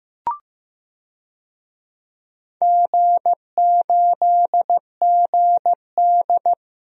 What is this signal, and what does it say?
Morse code, heard as a keyed tone, about 15 wpm, 700 Hz. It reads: G8GD
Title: DXCC-100 Call Signs - 4 Characters - Repeated 3x - 15wpm